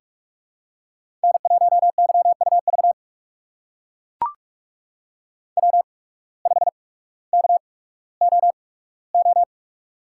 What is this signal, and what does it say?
Morse code, heard as a keyed tone, about 44 wpm, spaced as for 19 wpm, 700 Hz. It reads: N1YUV W5KOO
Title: USA Call Signs - 5 Characters - 8x Character Spacing - 45wpm